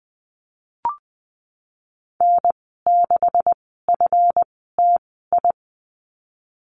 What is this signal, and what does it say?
Morse code, heard as a keyed tone, about 20 wpm, spaced as for 15 wpm, 700 Hz. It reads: N6FTI
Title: 173 Lesson 28g - Number 9 - Call Signs 20wpm